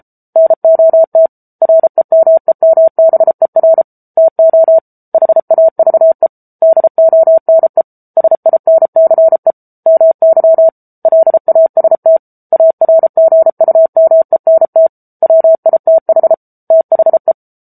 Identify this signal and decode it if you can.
Morse code, heard as a keyed tone, about 34 wpm, 650 Hz. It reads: NOT REMEMBER TO HAVE DONE SINCE MY LAST ARGUMENT WITH THE